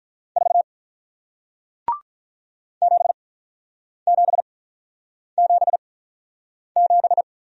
Morse code, heard as a keyed tone, about 47 wpm, spaced as for 9 wpm, 700 Hz. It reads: 4 7777